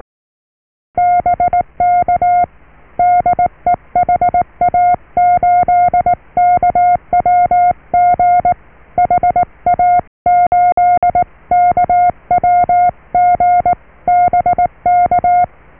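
Morse code, heard as a keyed tone, about 19 wpm, 710 Hz. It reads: BK DE HA8KWG HA8KWG BK
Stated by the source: Morse QSO on 14 MHz with a German station, received at RST 579.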